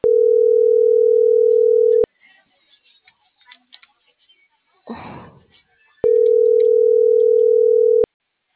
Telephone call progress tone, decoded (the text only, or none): ringback tone